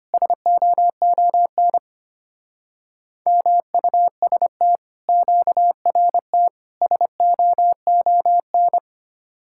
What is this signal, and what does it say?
Morse code, heard as a keyed tone, about 31 wpm, 700 Hz. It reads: SOON MUST QRT SOON